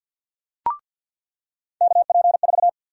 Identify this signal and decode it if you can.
Morse code, heard as a keyed tone, about 46 wpm, 700 Hz. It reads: KP4
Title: USA Call Sign Prefixes - 50wpm